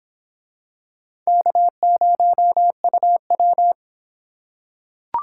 Morse code, heard as a keyed tone, about 25 wpm, 700 Hz. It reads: K0UW